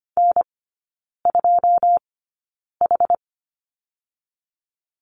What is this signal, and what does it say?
Morse code, heard as a keyed tone, about 25 wpm, spaced as for 9 wpm, 690 Hz. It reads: N2H
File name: Lesson 14e - Number 2 - Three Characters 25wpm